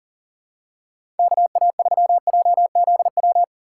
Morse code, heard as a keyed tone, about 40 wpm, 700 Hz. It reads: KA3JZW
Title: USA Call Signs - 6 Characters - 6x Character Spacing - 40wpm